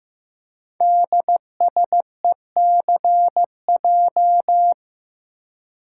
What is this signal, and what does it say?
Morse code, heard as a keyed tone, about 15 wpm, 700 Hz. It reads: DSECJ
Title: Code Groups - 5 Letters - 15wpm